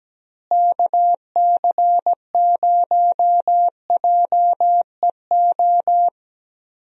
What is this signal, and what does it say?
Morse code, heard as a keyed tone, about 17 wpm, 700 Hz. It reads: KC0JEO
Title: USA Call Signs - 6 Characters - Repeated 3x - 17wpm